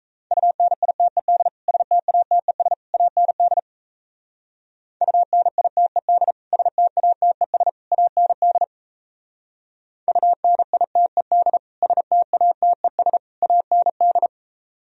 Morse code, heard as a keyed tone, about 35 wpm, 700 Hz. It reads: UNITED STATES AND UNITED STATES AND UNITED STATES AND
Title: Speed-Racing - Sets of 3 Words 28wpm